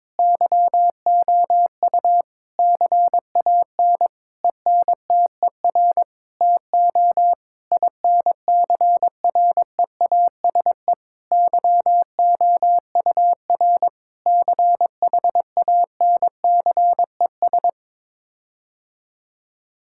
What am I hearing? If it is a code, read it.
Morse code, heard as a keyed tone, about 21 wpm, 700 Hz. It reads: YOU CAN ENTER TO INCREASE YOUR CHANCES
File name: practice / 22wpm / Sets of 8 Words - 22wpm